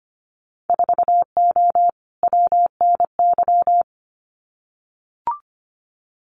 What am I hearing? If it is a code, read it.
Morse code, heard as a keyed tone, about 25 wpm, 690 Hz. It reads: 4O WNY
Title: Winter Field Day Exchanges Encore 25wpm